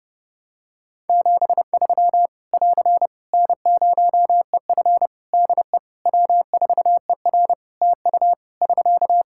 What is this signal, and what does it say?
Morse code, heard as a keyed tone, about 30 wpm, 700 Hz. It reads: 73 <AR> N0EF DE W4ER TU <SK>
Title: Practice QSO - Exchange 4 - Ending - 30wpm